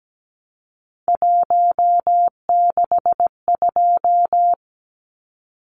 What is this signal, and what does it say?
Morse code, heard as a keyed tone, about 17 wpm, 700 Hz. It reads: 162
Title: Numbers - 3-Digit 17wpm